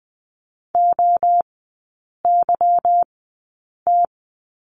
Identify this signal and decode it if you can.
Morse code, heard as a keyed tone, about 20 wpm, spaced as for 8 wpm, 700 Hz. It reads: OYT
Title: 145 Lesson 24e - Letter y - Three Characters 20wpm